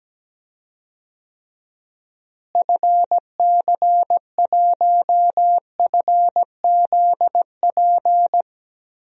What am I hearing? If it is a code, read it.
Morse code, heard as a keyed tone, about 17 wpm, 700 Hz. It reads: FC1FZP
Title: DXCC-100 Call Signs - 6 Characters - Repeated 3x - 17wpm